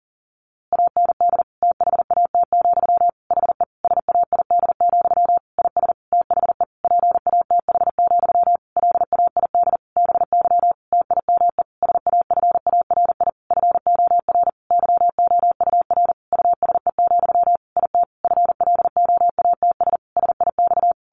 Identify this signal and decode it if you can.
Morse code, heard as a keyed tone, about 37 wpm, 700 Hz. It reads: AND THAT, HE SAID, IS THE PATH, LAID BY TIME SAFARI FOR YOUR USE, IT FLOATS SIX
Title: A Sound Of Thunder 40wpm